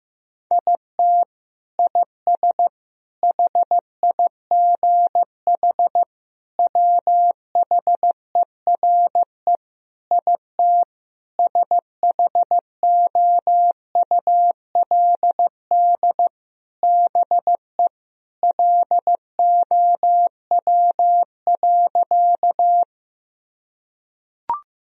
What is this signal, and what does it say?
Morse code, heard as a keyed tone, about 15 wpm, 700 Hz. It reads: IT IS HIGH WHERE IT SHOULD BE LOW.
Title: Sentences from Top 1000 Words - Review Part 1 - 15wpm